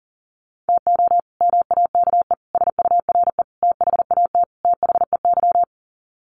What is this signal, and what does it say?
Morse code, heard as a keyed tone, about 40 wpm, 700 Hz. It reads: TO MAKE SURE THAT THEY
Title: Sets of 5 Words Encore2 40wpm